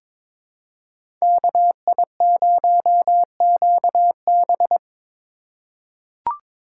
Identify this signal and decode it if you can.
Morse code, heard as a keyed tone, about 22 wpm, 700 Hz. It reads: KI0QB
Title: USA Call Signs - 5 Characters - 4x Character Spacing - 22wpm